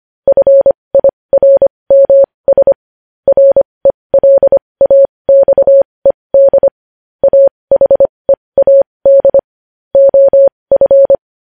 Morse code, heard as a keyed tone, about 25 wpm, spaced as for 20 wpm, 550 Hz. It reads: FIRMS RELAXED AHEAD OF
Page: news headlines in morse code at 20 wpm